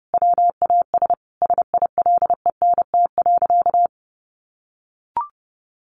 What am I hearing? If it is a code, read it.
Morse code, heard as a keyed tone, about 30 wpm, 700 Hz. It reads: WAS SILENT.
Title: A Sound Of Thunder 30wpm